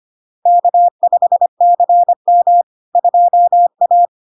Morse code, heard as a keyed tone, about 25 wpm, 700 Hz. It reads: K5CM 2A
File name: Log-in-CW-25wpm-700-Hz